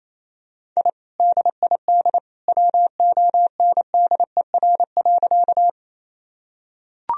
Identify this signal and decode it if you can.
Morse code, heard as a keyed tone, about 28 wpm, 700 Hz. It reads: I DID WONDER.
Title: Sentences from Top 1000 Words - Review Part 3 - 28wpm